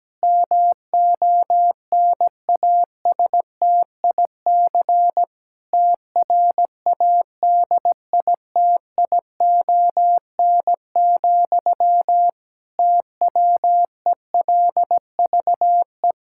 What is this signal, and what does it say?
Morse code, heard as a keyed tone, about 17 wpm, 700 Hz. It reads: MONASTIC TRADITION, TWELVE
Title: Pangram Challenge v2 - 17wpm